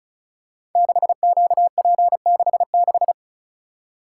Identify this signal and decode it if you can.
Morse code, heard as a keyed tone, about 36 wpm, 690 Hz. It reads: 6QP66
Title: Code Groups - 5 Letter:Numbers - 35wpm